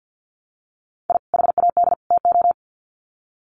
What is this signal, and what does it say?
Morse code, heard as a keyed tone, about 48 wpm, 700 Hz. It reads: I HAD TO
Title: Sets of 3 Words 50wpm